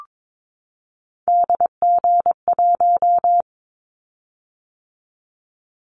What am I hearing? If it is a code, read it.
Morse code, heard as a keyed tone, about 22 wpm, 700 Hz. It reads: DG1
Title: Callsign Prefixes for Top 25 DXCC - 22wpm